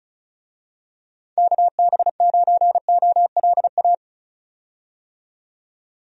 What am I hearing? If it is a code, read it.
Morse code, heard as a keyed tone, about 34 wpm, 700 Hz. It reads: KB9OLA
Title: USA Call Signs - 6 Characters - 35wpm